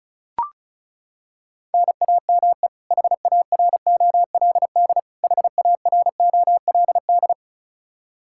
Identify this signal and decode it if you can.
Morse code, heard as a keyed tone, about 36 wpm, 700 Hz. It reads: NAME HAROLD HAROLD
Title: K1USN SST QSO Practice 35wpm